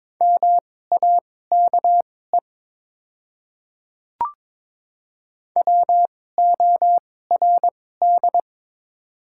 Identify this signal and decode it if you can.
Morse code, heard as a keyed tone, about 22 wpm, spaced as for 16 wpm, 700 Hz. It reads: MAKE WORD
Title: 2x Character Spacing - Top 100 Words - 22wpm